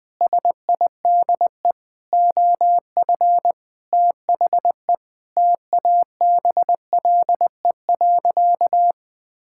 Morse code, heard as a keyed tone, about 20 wpm, 700 Hz. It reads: SIDE OF THE TABLE.
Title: Sentences with 6-Word Phrases - 20wpm